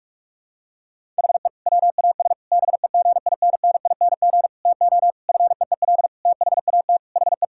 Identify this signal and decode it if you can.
Morse code, heard as a keyed tone, about 41 wpm, 700 Hz. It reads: HE WAS BEGINNING TO FEEL THAT HE
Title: Sentences from Top 1000 Words - Review Part 5 - 45wpm